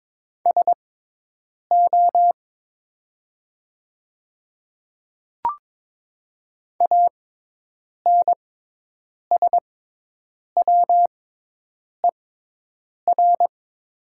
Morse code, heard as a keyed tone, about 22 wpm, spaced as for 8 wpm, 700 Hz. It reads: SO ANSWER
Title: 6x Character Spacing - Top 500 Words - 22wpm